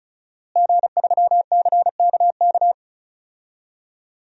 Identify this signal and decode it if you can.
Morse code, heard as a keyed tone, about 34 wpm, 700 Hz. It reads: G3CKK
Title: DXCC-100 Call Signs - 5 Characters - Repeated 2x - 35wpm